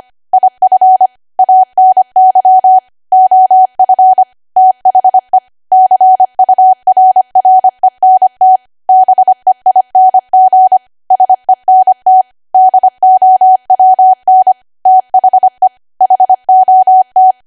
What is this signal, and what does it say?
Morse code, heard as a keyed tone, about 25 wpm, 750 Hz. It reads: IF ANY OF THE CURRENT BEING SENT DOWN THE HOT
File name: cwa / W1AW-Level3 / 25WPM